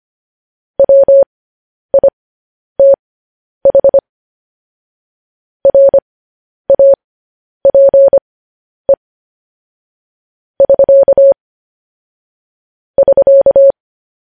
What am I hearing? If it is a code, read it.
Morse code, heard as a keyed tone, about 25 wpm, spaced as for 10 wpm, 550 Hz. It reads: WITH RAPE <SK> <SK>
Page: news headlines in morse code at 10 wpm